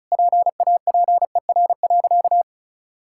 Morse code, heard as a keyed tone, about 36 wpm, 700 Hz. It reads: PAPER.
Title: Sentences from Top 1600 Words 35wpm